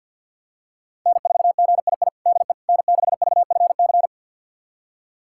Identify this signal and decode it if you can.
Morse code, heard as a keyed tone, about 52 wpm, 700 Hz. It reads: N4GII DE N6VFC
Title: USA Callsign DE Callsign - 5 Characters - 50wpm